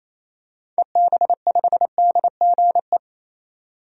Morse code, heard as a keyed tone, about 28 wpm, 700 Hz. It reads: EB5DGE
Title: DXCC-100 Call Signs - 6 Characters - 28wpm